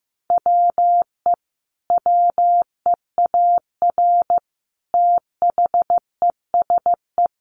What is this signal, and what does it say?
Morse code, heard as a keyed tone, about 15 wpm, 700 Hz. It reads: WE WEAR THESE
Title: A Sound Of Thunder 15wpm